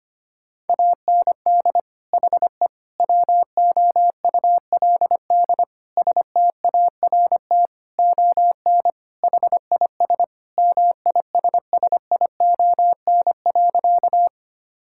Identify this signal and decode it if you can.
Morse code, heard as a keyed tone, about 25 wpm, 700 Hz. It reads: AND HE WOULD START ON HIS MISSION.
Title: Sentences from Top 1600 Words 25wpm